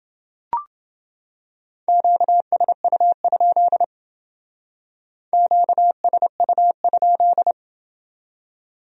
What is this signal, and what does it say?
Morse code, heard as a keyed tone, about 29 wpm, 700 Hz. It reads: QSU? QSU?